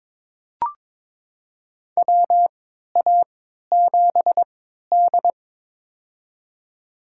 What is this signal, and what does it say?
Morse code, heard as a keyed tone, about 22 wpm, spaced as for 13 wpm, 700 Hz. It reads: WA7D
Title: USA Call Signs - 4 Characters - 3x Character Spacing - 22wpm